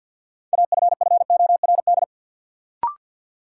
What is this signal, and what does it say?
Morse code, heard as a keyed tone, about 53 wpm, 700 Hz. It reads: AFFORD
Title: Sentences from Top 1100 Words - New Word Drill - 50wpm